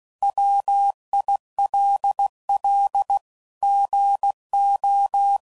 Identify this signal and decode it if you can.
Morse code, heard as a keyed tone, about 16 wpm, 800 Hz. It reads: WILL GO
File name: CW Code Groups @15WPM / Technical Subjects / AND Gate